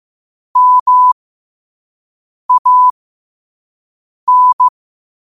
Morse code, heard as a keyed tone, about 15 wpm, spaced as for 5 wpm, 1000 Hz. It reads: MAN
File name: qotdfiles / 5wpm